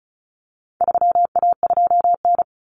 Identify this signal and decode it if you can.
Morse code, heard as a keyed tone, about 34 wpm, 700 Hz. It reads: 3A2N
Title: Common Cut Numbers - 4-Digit 35wpm